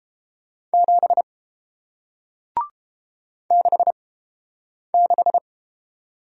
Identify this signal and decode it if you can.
Morse code, heard as a keyed tone, about 32 wpm, spaced as for 8 wpm, 700 Hz. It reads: 7 66